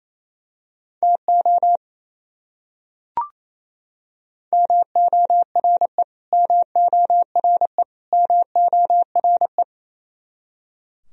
Morse code, heard as a keyed tone, about 28 wpm, 700 Hz. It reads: TO MORE MORE MORE
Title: Top 100 Words Repeated 28wpm